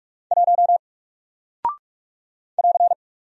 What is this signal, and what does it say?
Morse code, heard as a keyed tone, about 44 wpm, 700 Hz. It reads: 1 <AR>
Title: Single Character (Letter:Number:Punctuation:Prosign) Alt 45wpm